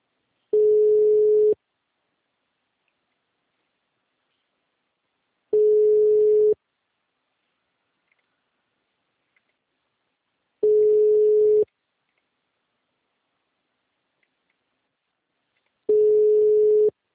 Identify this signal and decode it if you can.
Ringback tone (Europe, 425 Hz).